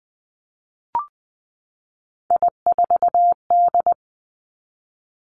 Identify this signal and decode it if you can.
Morse code, heard as a keyed tone, about 20 wpm, 700 Hz. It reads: I4D
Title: Three Challenging Letter-Number Combinations 20wpm